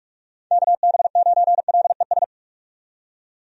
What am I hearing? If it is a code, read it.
Morse code, heard as a keyed tone, about 46 wpm, 700 Hz. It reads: KB9LES
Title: USA Call Signs - 6 Characters - 45wpm